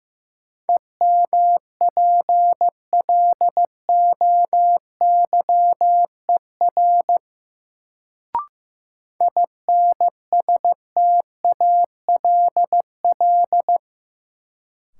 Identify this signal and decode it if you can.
Morse code, heard as a keyed tone, about 15 wpm, 700 Hz. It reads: EMPLOYER INSTALL
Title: Sentences from Top 1600 Words - New Word Drill - 15wpm